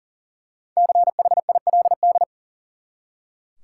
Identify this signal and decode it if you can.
Morse code, heard as a keyed tone, about 40 wpm, 700 Hz. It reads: CHILD
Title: Five Letter Words - 40wpm